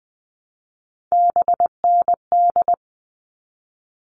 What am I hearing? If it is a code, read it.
Morse code, heard as a keyed tone, about 20 wpm, 700 Hz. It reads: BND
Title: All Cut Numbers - 3-Digit 20wpm